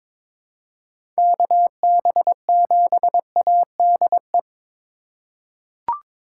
Morse code, heard as a keyed tone, about 21 wpm, 700 Hz. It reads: KB7ADE
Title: USA Call Signs - 6 Characters - 2x Character Spacing - 22wpm